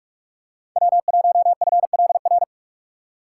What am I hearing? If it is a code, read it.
Morse code, heard as a keyed tone, about 46 wpm, 700 Hz. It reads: W1FLR
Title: USA Call Signs - 2x Character Spacing - 45wpm